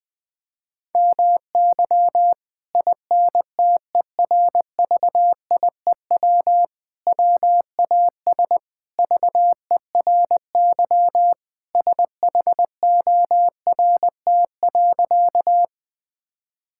Morse code, heard as a keyed tone, about 20 wpm, 700 Hz. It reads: MY INTERVIEW WAS VERY SHORT.
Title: Sentences from Top 1600 Words 20wpm